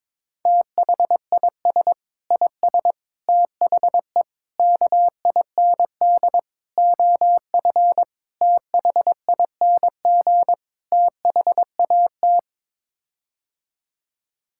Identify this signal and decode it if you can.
Morse code, heard as a keyed tone, about 22 wpm, 700 Hz. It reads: THIS IS THE KIND OF THING THAT
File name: practice / 22wpm / Sets of 7 Words - 22wpm